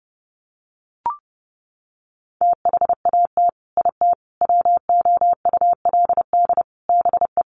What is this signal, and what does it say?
Morse code, heard as a keyed tone, about 30 wpm, 700 Hz. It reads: THAT IT WOULD BE
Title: Sets of 4 Words 30wpm